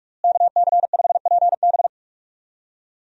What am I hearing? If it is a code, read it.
Morse code, heard as a keyed tone, about 46 wpm, 700 Hz. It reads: KC5PB
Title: USA Call Signs - 5 Characters - Repeated 3x - 45wpm